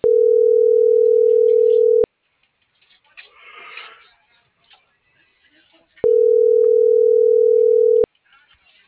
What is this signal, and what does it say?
Ringback tone (North America), 440 + 480 Hz.